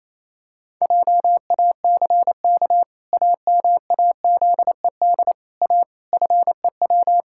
Morse code, heard as a keyed tone, about 28 wpm, 700 Hz. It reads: JACK AMAZED A FEW
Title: Pangram Challenge v2 - 28wpm